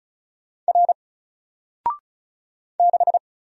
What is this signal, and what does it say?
Morse code, heard as a keyed tone, about 36 wpm, 700 Hz. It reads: R 6
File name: Single-Letter-Number-35wpm